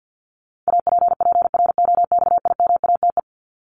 Morse code, heard as a keyed tone, about 50 wpm, 700 Hz. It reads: APPROXIMATE
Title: LY Suffix - Top 100 Words with Base Words 50wpm